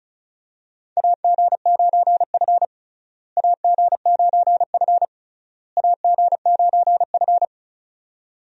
Morse code, heard as a keyed tone, about 34 wpm, 700 Hz. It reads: AG9F AG9F AG9F
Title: USA Call Signs - 4 Characters - Repeated 3x - 35wpm